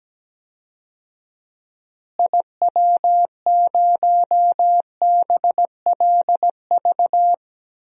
Morse code, heard as a keyed tone, about 17 wpm, 700 Hz. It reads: IW0BLV
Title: DXCC-100 Call Signs - 6 Characters - Repeated 2x - 17wpm